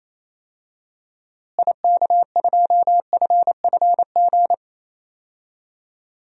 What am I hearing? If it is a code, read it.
Morse code, heard as a keyed tone, about 27 wpm, 700 Hz. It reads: IK2FFG
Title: DXCC-100 Call Signs - 6 Characters - Repeated 2x - 28wpm